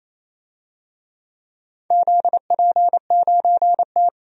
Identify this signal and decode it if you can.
Morse code, heard as a keyed tone, about 28 wpm, 700 Hz. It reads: ZP9T
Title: DXCC-100 Call Signs - 4 Characters - 28wpm